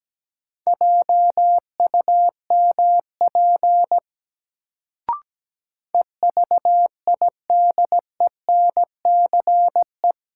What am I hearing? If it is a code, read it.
Morse code, heard as a keyed tone, about 17 wpm, 700 Hz. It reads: JUMP EVIDENCE
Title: Sentences from Top 1300 Words - New Word Drill - 17wpm